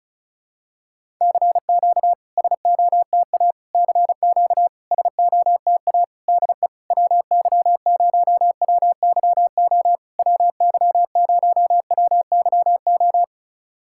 Morse code, heard as a keyed tone, about 36 wpm, 660 Hz. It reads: CQ SOTA CQ SOTA DE WY0WYO WY0WYO
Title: SOTA QSO Practice 35wpm